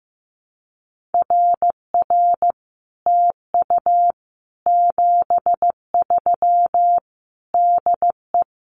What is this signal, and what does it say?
Morse code, heard as a keyed tone, about 15 wpm, 700 Hz. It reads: RR TU 73 DE
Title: Common QSO Phrases Encore 15wpm